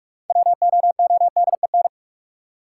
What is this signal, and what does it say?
Morse code, heard as a keyed tone, about 42 wpm, 700 Hz. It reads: WOODEN
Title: Six Letter Words - 45wpm